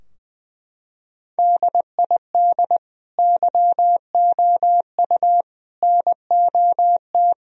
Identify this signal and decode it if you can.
Morse code, heard as a keyed tone, about 19 wpm, 700 Hz. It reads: DID YOU NOT
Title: Sentences from Top 1000 Words - Review Part 4 - 20wpm